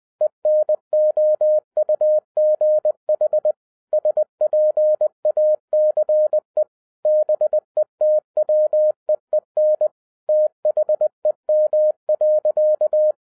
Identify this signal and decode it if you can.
Morse code, heard as a keyed tone, about 20 wpm, 610 Hz. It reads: ENOUGH SPACE BETWEEN THEM.
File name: 2011-10-mp3 / n13-TIPS FROM MY 34 YEARS OF EXPERIENCE 20wpm